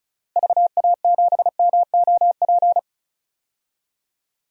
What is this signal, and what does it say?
Morse code, heard as a keyed tone, about 36 wpm, 700 Hz. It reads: VA7MOP
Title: DXCC-100 Call Signs - 6 Characters - Repeated 3x - 35wpm